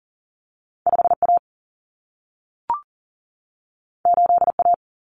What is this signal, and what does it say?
Morse code, heard as a keyed tone, about 40 wpm, 700 Hz. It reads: 5A 8A
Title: Common Cut Numbers - 2-Digit 40wpm